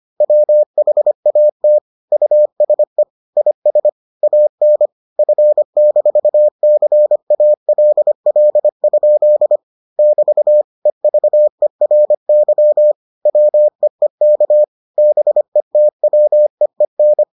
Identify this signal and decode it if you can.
Morse code, heard as a keyed tone, about 25 wpm, 600 Hz. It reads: WHAT USE IS AN F-CALL? <BT> EVERY WEEK BETWEEN